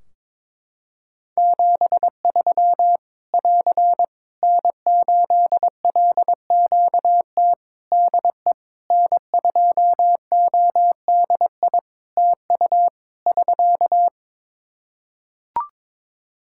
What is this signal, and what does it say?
Morse code, heard as a keyed tone, about 21 wpm, 700 Hz. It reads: 73 <AR> N8LQT DE N2ODI TU <SK>
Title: K1USN SST QSO Practice 22wpm